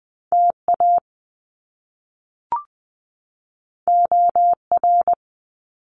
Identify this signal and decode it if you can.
Morse code, heard as a keyed tone, about 20 wpm, 700 Hz. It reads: TA OR
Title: Top Two Letter Combinations in Words 20wpm